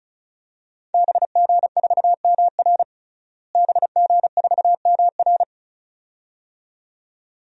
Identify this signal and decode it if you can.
Morse code, heard as a keyed tone, about 36 wpm, 700 Hz. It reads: BG4MR BG4MR
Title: DXCC-100 Call Signs - 5 Characters - Repeated 2x - 35wpm